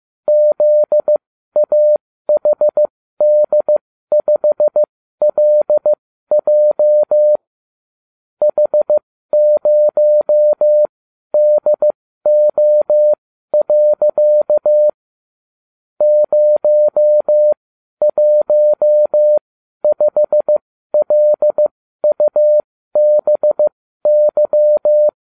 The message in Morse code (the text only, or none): ZAHD5LJ H0DO. 015LUBY